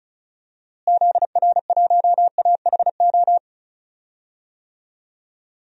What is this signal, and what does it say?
Morse code, heard as a keyed tone, about 36 wpm, 700 Hz. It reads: ZR1AHO
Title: DXCC-100 Call Signs - 6 Characters - Repeated 2x - 35wpm